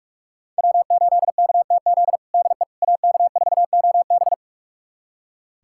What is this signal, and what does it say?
Morse code, heard as a keyed tone, about 46 wpm, 700 Hz. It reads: W8KTZ DE AK4OB
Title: USA Callsign DE Callsign - 5 Characters - 45wpm